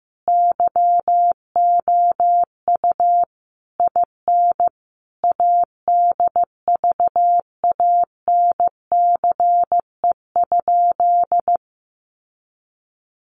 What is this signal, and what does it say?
Morse code, heard as a keyed tone, about 15 wpm, 700 Hz. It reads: YOU IN ADVANCE?
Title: Sentences from Top 1400 Words 15wpm